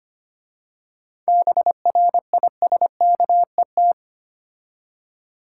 Morse code, heard as a keyed tone, about 25 wpm, 700 Hz. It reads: BRISKET
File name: practice / 25wpm / Difficult Words with Hints - 25wpm